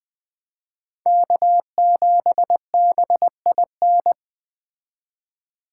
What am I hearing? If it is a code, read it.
Morse code, heard as a keyed tone, about 21 wpm, 700 Hz. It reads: K7BIN
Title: Speed Racing - USA 1x3 Call Signs 17wpm